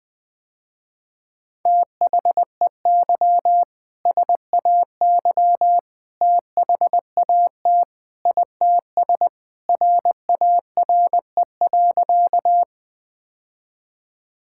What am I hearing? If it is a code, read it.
Morse code, heard as a keyed tone, about 20 wpm, 700 Hz. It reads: THEY SAY THAT ITS RARE.
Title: Sentences with 2-Word Phrases - Spoken Beginning - 20wpm